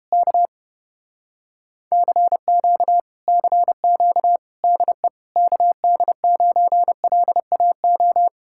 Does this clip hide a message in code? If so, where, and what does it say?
Yes: Morse code from 0.1 to 8.3 s, heard as a keyed tone, about 31 wpm, 700 Hz. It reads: K CQ CQ DE KD9LAO